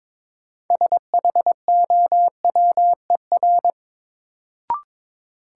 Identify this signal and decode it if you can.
Morse code, heard as a keyed tone, about 22 wpm, 700 Hz. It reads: SHOWER
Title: Sentences from Top 1600 Words - New Word Drill Alt - 22wpm